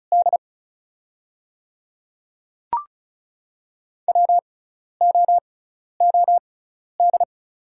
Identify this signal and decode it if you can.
Morse code, heard as a keyed tone, about 36 wpm, spaced as for 12 wpm, 700 Hz. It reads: D WOOD
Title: 6x Character Spacing - Top 500 Words - 35wpm